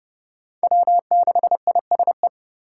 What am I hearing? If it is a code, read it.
Morse code, heard as a keyed tone, about 30 wpm, 700 Hz. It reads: W6ISE
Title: USA Call Signs - 5 Characters - 2x Character Spacing - 30wpm